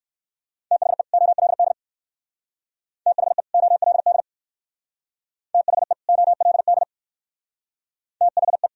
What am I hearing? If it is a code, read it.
Morse code, heard as a keyed tone, about 60 wpm, 700 Hz. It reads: THE OLD THE OLD THE OLD THE